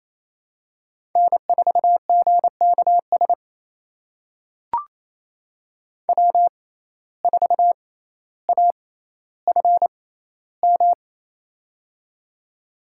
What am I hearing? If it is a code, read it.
Morse code, heard as a keyed tone, about 28 wpm, spaced as for 14 wpm, 700 Hz. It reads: N4GKS W4AFM
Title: USA Call Signs - 5 Characters - 6x Character Spacing - 28wpm